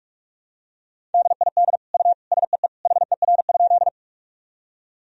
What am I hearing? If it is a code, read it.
Morse code, heard as a keyed tone, about 46 wpm, 700 Hz. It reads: DID U SEE HER?
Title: Sentences from Top 100 Words and Abbreviations 45wpm